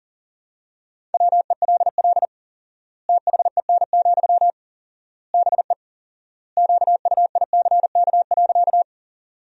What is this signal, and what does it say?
Morse code, heard as a keyed tone, about 40 wpm, 700 Hz. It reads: WELL THEN, BE QUICK.
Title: 4x Word-Spacing - Sentences from Top 500 Words - 40wpm